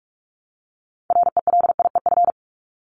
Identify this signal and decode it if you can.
Morse code, heard as a keyed tone, about 44 wpm, 700 Hz. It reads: RELIEF
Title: Top 1501-1750 Words 45wpm